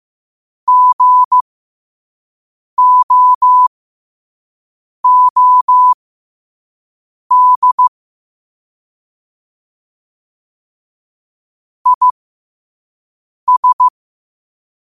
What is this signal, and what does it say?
Morse code, heard as a keyed tone, about 15 wpm, spaced as for 5 wpm, 1000 Hz. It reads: GOOD IS